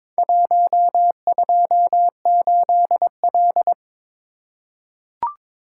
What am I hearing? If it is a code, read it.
Morse code, heard as a keyed tone, about 22 wpm, 700 Hz. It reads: 128L
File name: practice / 22wpm / Four Letter-Number Combinations - 22wpm